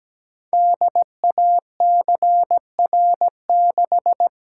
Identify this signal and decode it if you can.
Morse code, heard as a keyed tone, about 17 wpm, 700 Hz. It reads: DACR6
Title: Code Groups - 5 Letter:Numbers - 17wpm